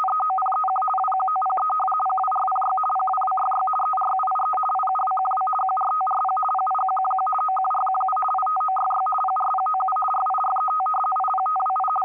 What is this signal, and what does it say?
RTTY, about 50 baud. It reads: EOPEQ WQOPE PWRPY PWWYY ERUIE IUWQW QRTRP PIQQQ OQRQU ITOYY PRPOU RIRET IPTWI
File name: M42b-10483khz-K4MT-1500-to-1507